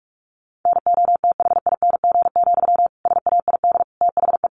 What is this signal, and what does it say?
Morse code, heard as a keyed tone, about 44 wpm, 700 Hz. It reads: NOTHING, SAID THE
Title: Sentences from Top 1300 Words 45wpm